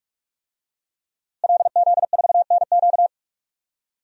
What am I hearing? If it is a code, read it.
Morse code, heard as a keyed tone, about 41 wpm, 700 Hz. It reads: LZ4NQ
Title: DXCC-100 Call Signs - 5 Characters - Repeated 3x - 45wpm